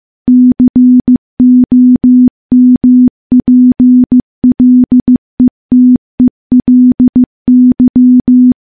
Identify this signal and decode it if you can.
Morse code, heard as a keyed tone, about 15 wpm, 250 Hz. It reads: COMPLETELY
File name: morse-training-15WPM